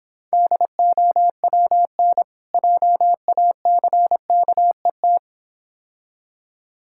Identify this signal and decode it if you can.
Morse code, heard as a keyed tone, about 26 wpm, 700 Hz. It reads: DOWN JACKET